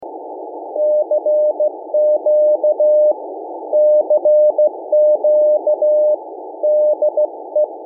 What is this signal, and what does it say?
Morse code, heard as a keyed tone, about 15 wpm, 600 Hz. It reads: CQ CQ DE
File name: MainWeb / Rocky 7024-CW